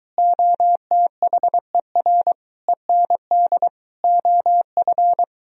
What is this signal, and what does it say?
Morse code, heard as a keyed tone, about 22 wpm, 700 Hz. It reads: OTHER END OF